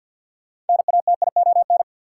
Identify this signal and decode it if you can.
Morse code, heard as a keyed tone, about 53 wpm, 700 Hz. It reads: NATION
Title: Sentences from Top 900 Words - New Word Drill - 50wpm